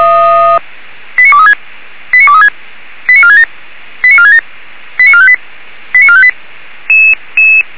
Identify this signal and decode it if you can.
Five-tone selective call (ZVEI), heard as quick runs of five tones, tones of about 80 ms. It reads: 89236 89236 89467 89467 89458 89458
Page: /464/SDR/ZVEI/89377/